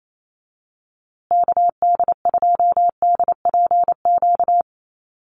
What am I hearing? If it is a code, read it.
Morse code, heard as a keyed tone, about 28 wpm, 700 Hz. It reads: KD2DPQ
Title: USA Call Signs Repeated 2x 28wpm